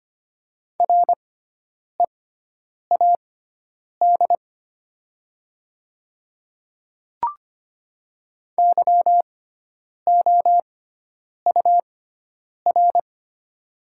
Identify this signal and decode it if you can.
Morse code, heard as a keyed tone, about 25 wpm, spaced as for 9 wpm, 700 Hz. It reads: READ YOUR
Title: 6x Character Spacing - Top 200 Words - 25wpm